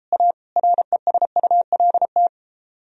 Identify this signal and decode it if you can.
Morse code, heard as a keyed tone, about 31 wpm, 700 Hz. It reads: A RESULT